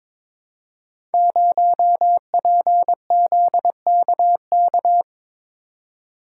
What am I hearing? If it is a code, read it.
Morse code, heard as a keyed tone, about 22 wpm, 700 Hz. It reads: 0PZKK
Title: Code Groups - 5 Letter:Numbers - 22wpm